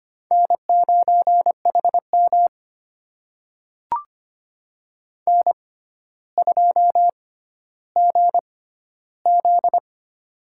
Morse code, heard as a keyed tone, about 25 wpm, spaced as for 13 wpm, 700 Hz. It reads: N9HM N2GZ
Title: USA Call Signs - 4 Characters - 6x Character Spacing - 25wpm